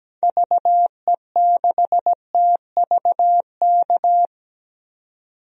Morse code, heard as a keyed tone, about 17 wpm, 700 Hz. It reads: VE6TVK
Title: DXCC-100 Call Signs - 6 Characters - 17wpm